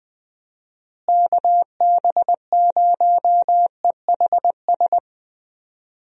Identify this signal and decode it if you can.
Morse code, heard as a keyed tone, about 20 wpm, 700 Hz. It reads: KB0EHS
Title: USA Call Signs - 6 Characters - Repeated 2x - 20wpm